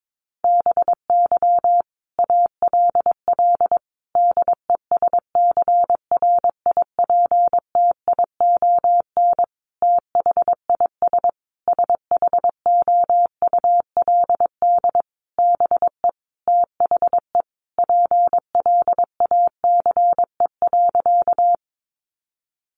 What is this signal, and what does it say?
Morse code, heard as a keyed tone, about 22 wpm, 700 Hz. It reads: BY ALL DESCRIPTION THIS SHOULD BE THE PLACE.
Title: Sentences from Top 1300 Words 22wpm